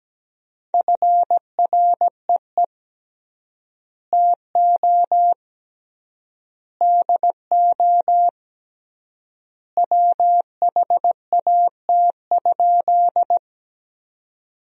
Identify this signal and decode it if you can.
Morse code, heard as a keyed tone, about 17 wpm, 700 Hz. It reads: FREE TO DO WHAT?
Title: 3x Word-Spacing - Sentences from Top 500 Words - 17wpm